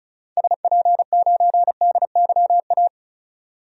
Morse code, heard as a keyed tone, about 36 wpm, 700 Hz. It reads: SP9DYA
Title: DXCC-100 Call Signs - 6 Characters - 35wpm